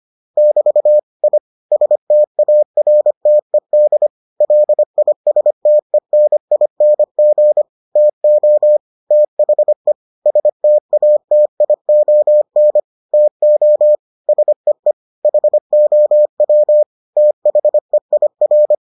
Morse code, heard as a keyed tone, about 25 wpm, 600 Hz. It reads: <BT> I STARTED LISTENING TO THE STATION TO SEE HOW THEIR